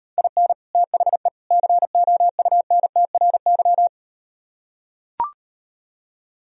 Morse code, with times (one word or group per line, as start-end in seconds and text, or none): 0.2-0.5 IN
0.8-1.3 THE
1.5-3.9 COUNTRY